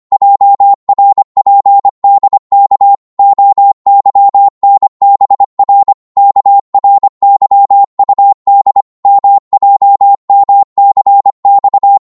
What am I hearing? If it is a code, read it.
Morse code, heard as a keyed tone, about 25 wpm, 800 Hz. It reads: JRPDK OYNBR KRYUD MJMCX